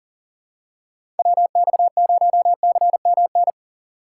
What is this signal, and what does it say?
Morse code, heard as a keyed tone, about 39 wpm, 700 Hz. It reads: WX0CMN
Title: USA Call Signs - 4x Character Spacing - 40wpm